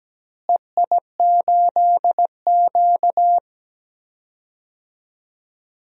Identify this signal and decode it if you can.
Morse code, heard as a keyed tone, about 17 wpm, 700 Hz. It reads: EI8Q
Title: DXCC-100 Call Signs - 4 Characters - Repeated 2x - 17wpm